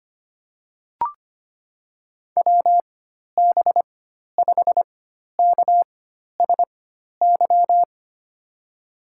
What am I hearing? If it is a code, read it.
Morse code, heard as a keyed tone, about 25 wpm, spaced as for 12 wpm, 700 Hz. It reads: WB5KSY
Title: USA Call Signs - 6 Characters - 4x Character Spacing - 25wpm